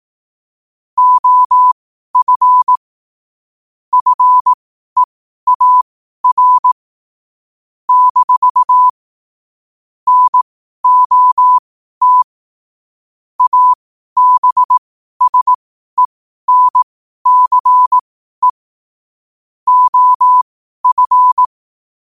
Morse code, heard as a keyed tone, about 18 wpm, spaced as for 12 wpm, 1000 Hz. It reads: OF FEAR - NOT ABSENCE OF
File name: qotdfiles / 12wpm